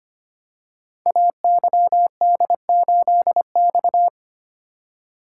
Morse code, heard as a keyed tone, about 25 wpm, 700 Hz. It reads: AYD8X
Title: Code Groups - 5 Letter:Numbers - 25wpm